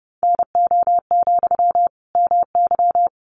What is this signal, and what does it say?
Morse code, heard as a keyed tone, about 30 wpm, 700 Hz. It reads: NO, MY
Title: Sentences from Top 400 Words 30wpm